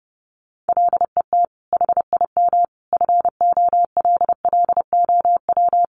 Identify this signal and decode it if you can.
Morse code, heard as a keyed tone, about 30 wpm, 700 Hz. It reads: LET HIM FOLLOW
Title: Sentences from Top 300 Words 30wpm